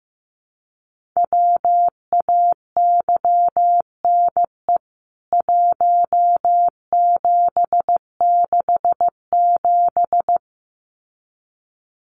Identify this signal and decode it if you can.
Morse code, heard as a keyed tone, about 15 wpm, 700 Hz. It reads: WAYNE 1767
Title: CWT Exchange 15wpm